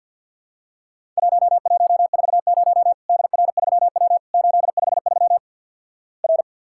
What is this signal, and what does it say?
Morse code, heard as a keyed tone, about 50 wpm, 700 Hz. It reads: 1140 DR3W 853 R
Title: Worked All Europe DX Contest - QTC Exchange - 50wpm